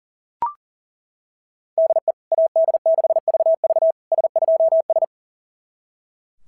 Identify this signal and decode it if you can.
Morse code, heard as a keyed tone, about 40 wpm, 690 Hz. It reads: DE AD6VV S2S
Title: SOTA QSO Practice 40wpm